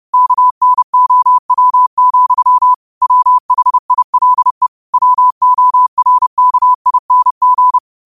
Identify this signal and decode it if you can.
Morse code, heard as a keyed tone, about 30 wpm, 1000 Hz. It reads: KNOW, WHILE WORKING
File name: qotdfiles / 30wpm